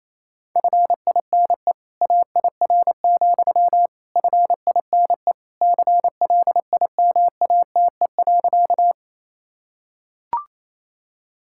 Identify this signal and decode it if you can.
Morse code, heard as a keyed tone, about 28 wpm, 700 Hz. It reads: FINE AIR, FINE CLIMATE.
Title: Sentences from Top 1600 Words 28wpm